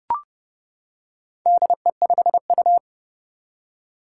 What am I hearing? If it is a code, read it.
Morse code, heard as a keyed tone, about 30 wpm, 700 Hz. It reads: DE5U
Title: Four Challenging Letter-Number Combinations 30wpm